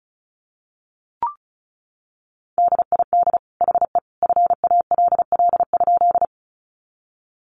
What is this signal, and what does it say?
Morse code, heard as a keyed tone, about 34 wpm, 700 Hz. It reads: DID HE FALL?
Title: Sentences from Top 500 Words 35wpm